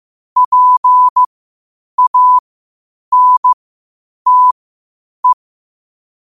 Morse code, heard as a keyed tone, about 15 wpm, spaced as for 8 wpm, 1000 Hz. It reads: PANTE
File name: qotdfiles / 8wpm